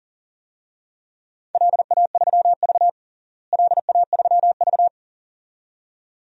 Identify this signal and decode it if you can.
Morse code, heard as a keyed tone, about 40 wpm, 700 Hz. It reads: LA3V LA3V
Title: DXCC-100 Call Signs - 4 Characters - Repeated 2x - 40wpm